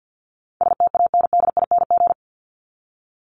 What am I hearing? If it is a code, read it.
Morse code, heard as a keyed tone, about 50 wpm, 700 Hz. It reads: STANDING
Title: ING Suffix - Top 100 Words with Base Words 50wpm